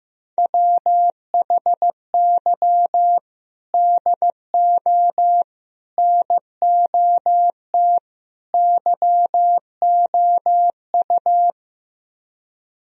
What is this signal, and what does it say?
Morse code, heard as a keyed tone, about 15 wpm, 700 Hz. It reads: WHY DO NOT YOU
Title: Speed-Racing - Sets of 4 Words 15wpm